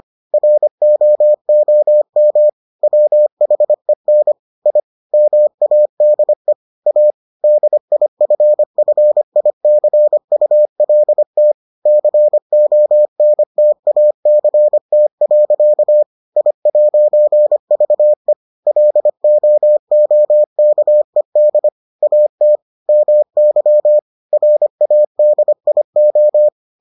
Morse code, heard as a keyed tone, about 25 wpm, 600 Hz. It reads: ROOM WHEN I MADE A DIFFICULT CONTACT. I'VE LOOKED AT MY RADIO